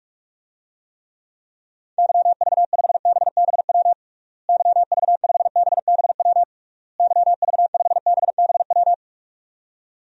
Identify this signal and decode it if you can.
Morse code, heard as a keyed tone, about 46 wpm, 700 Hz. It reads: YV5BBW YV5BBW YV5BBW
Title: DXCC-100 Call Signs - 6 Characters - Repeated 3x - 45wpm